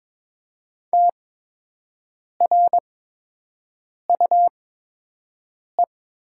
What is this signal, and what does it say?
Morse code, heard as a keyed tone, about 22 wpm, spaced as for 6 wpm, 700 Hz. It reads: TRUE
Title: 8x Character Spacing - Top 500 Words - 22wpm